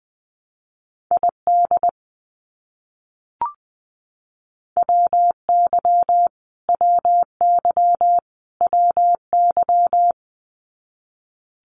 Morse code, heard as a keyed tone, about 20 wpm, 700 Hz. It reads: ID WY WY WY
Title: US States Repeated 3x 20wpm